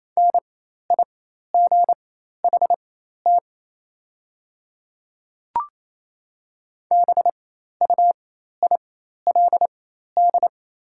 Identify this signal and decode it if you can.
Morse code, heard as a keyed tone, about 27 wpm, spaced as for 13 wpm, 700 Hz. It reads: NIGHT BUILD
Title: 4x Character Spacing - Top 500 Words - 28wpm